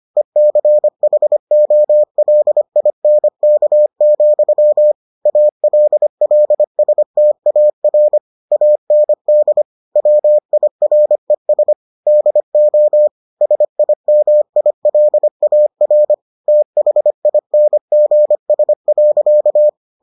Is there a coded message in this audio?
Yes: Morse code from 0.2 to 19.7 s, heard as a keyed tone, about 25 wpm, 600 Hz. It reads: ECHOLINK, ALLSTAR AND WIRES DO SIMILAR THINGS.